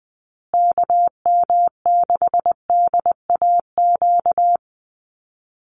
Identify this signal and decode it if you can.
Morse code, heard as a keyed tone, about 20 wpm, 700 Hz. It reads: KM6DAQ
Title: USA Call Signs Repeated 3x 20wpm